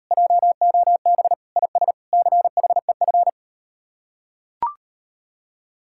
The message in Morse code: JOB IS CHEF

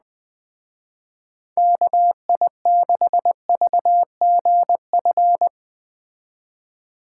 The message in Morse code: KI6VGF

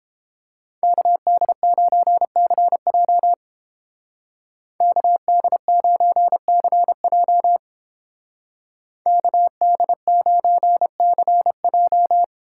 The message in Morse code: KD9CJ KD9CJ KD9CJ

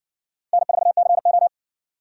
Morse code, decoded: N4CO

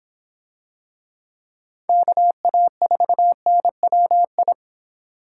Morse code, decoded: KA4NWI

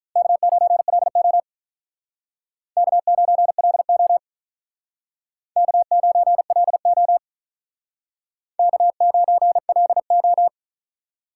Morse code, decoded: K9LO K9LO K9LO K9LO

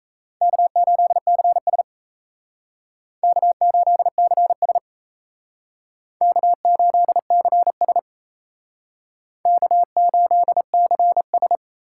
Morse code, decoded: K8CS K8CS K8CS K8CS

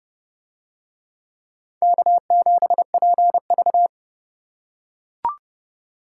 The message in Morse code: K7PV

